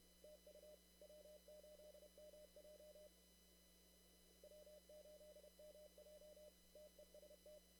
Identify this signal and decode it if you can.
Morse code, heard as a keyed tone, about 31 wpm, 600 Hz, with noise. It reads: TU W8MJ W8MJ TEST
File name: CW / kt5j wpx cw-op-k5pi